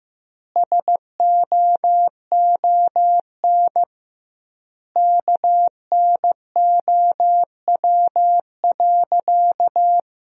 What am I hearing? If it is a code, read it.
Morse code, heard as a keyed tone, about 15 wpm, 700 Hz. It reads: SOON KNOW.
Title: 2x Word-Spacing - Sentences from Top 400 Words - 15wpm